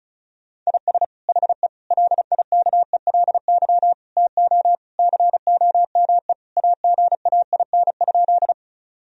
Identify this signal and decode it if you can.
Morse code, heard as a keyed tone, about 36 wpm, 700 Hz. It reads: IS HE LIKELY TO COME AGAIN?
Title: Sentences from Top 1000 Words - Review Part 4 - 35wpm